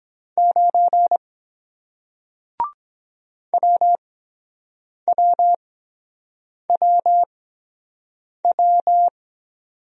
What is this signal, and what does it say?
Morse code, heard as a keyed tone, about 23 wpm, spaced as for 7 wpm, 700 Hz. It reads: 9 WWWW